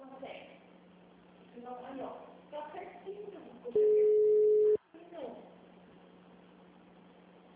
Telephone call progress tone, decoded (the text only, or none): ringback tone